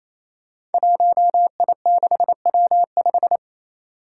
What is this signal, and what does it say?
Morse code, heard as a keyed tone, about 29 wpm, 700 Hz. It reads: 1I6W5
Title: Five Challenging Letter-Number Combinations 28wpm